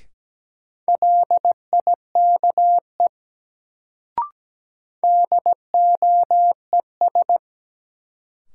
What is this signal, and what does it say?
Morse code, heard as a keyed tone, about 17 wpm, 700 Hz. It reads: LIKE DOES
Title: Sets of 2 Words - New Word Drill - 17wpm